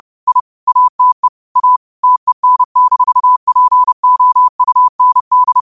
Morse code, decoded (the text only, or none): I ATE A TEN-POUND